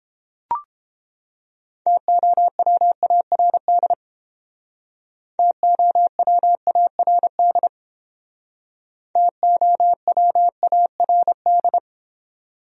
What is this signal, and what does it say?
Morse code, heard as a keyed tone, about 30 wpm, 700 Hz. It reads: TOWARD TOWARD TOWARD